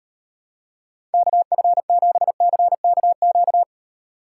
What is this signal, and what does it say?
Morse code, heard as a keyed tone, about 38 wpm, 700 Hz. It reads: KF7CKQ